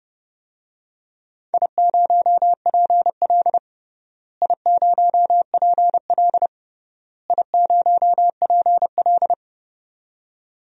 Morse code, heard as a keyed tone, about 30 wpm, 700 Hz. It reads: I0PL I0PL I0PL
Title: DXCC-100 Call Signs - 4 Characters - Repeated 3x - 30wpm